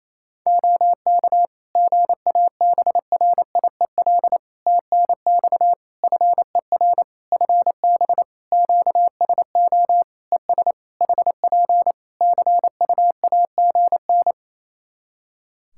Morse code, heard as a keyed tone, about 28 wpm, 700 Hz. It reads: OK GABRIEL TNX FER FB QSO ES HP CUAGN
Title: Practice QSO - Exchange 4 - Ending - 28wpm